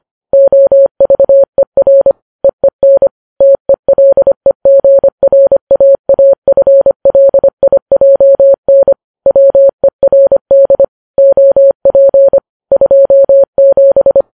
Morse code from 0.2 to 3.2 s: OVER EEN